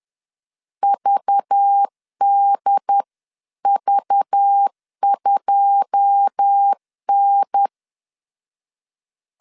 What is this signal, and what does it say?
Morse code, heard as a keyed tone, about 10 wpm, 800 Hz. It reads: VDV2N
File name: CW / DET / 1 9 1500C S A 10wpm 4511